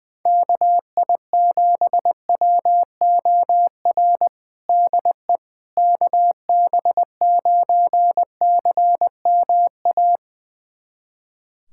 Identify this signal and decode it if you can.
Morse code, heard as a keyed tone, about 20 wpm, 700 Hz. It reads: KI7WOR DE KB9CMA
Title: USA Callsign DE Callsign - 6 Characters - 20wpm